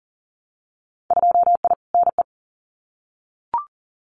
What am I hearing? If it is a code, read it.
Morse code, heard as a keyed tone, about 40 wpm, 700 Hz. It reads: 2I NE
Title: Winter Field Day Exchanges Encore 40wpm